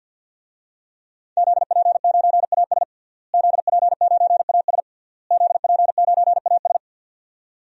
Morse code, heard as a keyed tone, about 50 wpm, 700 Hz. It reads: ZP9AS ZP9AS ZP9AS
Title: DXCC-100 Call Signs - 5 Characters - Repeated 3x - 50wpm